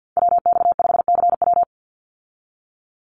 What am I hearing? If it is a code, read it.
Morse code, heard as a keyed tone, about 49 wpm, 700 Hz. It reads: WX5CW
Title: Call Signs Repeated 3x - 50wpm